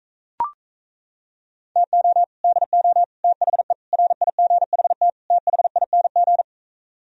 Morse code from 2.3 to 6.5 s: DO THE RIGHT THING

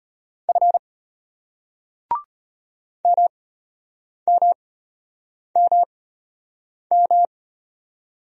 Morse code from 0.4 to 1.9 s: F

